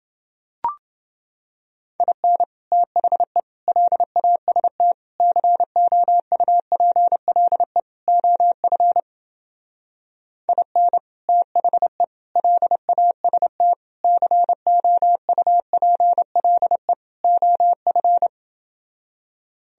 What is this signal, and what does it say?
Morse code, heard as a keyed tone, about 29 wpm, 700 Hz. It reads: IN THE LAST COUPLE OF IN THE LAST COUPLE OF